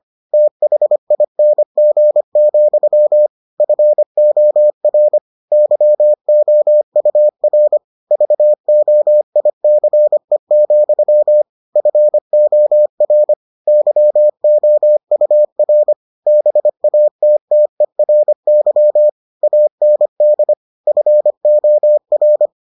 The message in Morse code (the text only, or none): THING, FOR YOUR VOICE, FOR YOUR BATTERY AND FOR